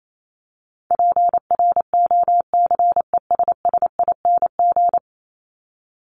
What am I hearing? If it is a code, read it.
Morse code, heard as a keyed tone, about 28 wpm, 700 Hz. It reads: PROCESSING
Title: ING Suffix - Top 100 Words with Base Words 28wpm